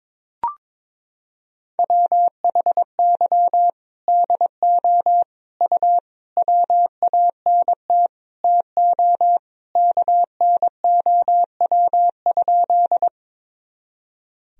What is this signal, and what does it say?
Morse code, heard as a keyed tone, about 22 wpm, 700 Hz. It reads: WHY DO U WANT TO KNOW?
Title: Sentences from Top 200 Words and Abbreviations 22wpm